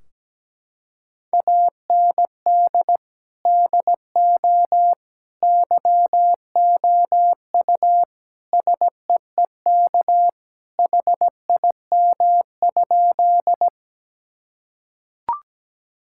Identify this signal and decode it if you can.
Morse code, heard as a keyed tone, about 17 wpm, 700 Hz. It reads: AND DO YOU SEEK HIM?
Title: Sentences from Top 1600 Words 17wpm